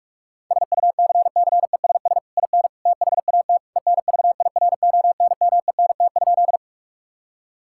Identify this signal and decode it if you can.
Morse code, heard as a keyed tone, about 46 wpm, 700 Hz. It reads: SUCCESS IN THAT ENVIRONMENT?
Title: Sentences from Top 1000 Words - Review Part 5 - 45wpm